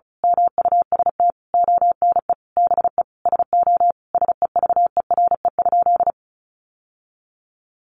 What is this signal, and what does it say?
Morse code, heard as a keyed tone, about 34 wpm, 700 Hz. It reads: MUST ONE BE SO SEVERE?
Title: Sentences from Top 1400 Words 35wpm